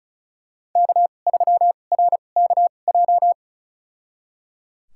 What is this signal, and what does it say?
Morse code, heard as a keyed tone, about 36 wpm, spaced as for 25 wpm, 700 Hz. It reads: K3RKJ
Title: USA Call Signs - 5 Characters - 2x Character Spacing - 35wpm